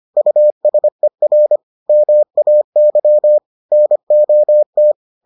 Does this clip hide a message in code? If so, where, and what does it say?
Yes: Morse code from 0.2 to 4.9 s, heard as a keyed tone, about 25 wpm, 600 Hz. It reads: USER MAY NOT